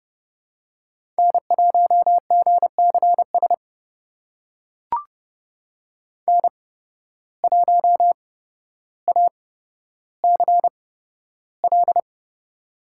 Morse code, heard as a keyed tone, about 30 wpm, spaced as for 13 wpm, 700 Hz. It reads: N1GCS N1ACL
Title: USA Call Signs - 5 Characters - 8x Character Spacing - 30wpm